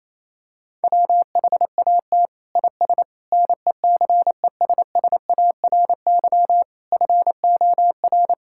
Morse code, heard as a keyed tone, about 28 wpm, 700 Hz. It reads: WHAT IS NECESSARY FOR
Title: Sentences from Top 1000 Words - Review Part 5 - 28wpm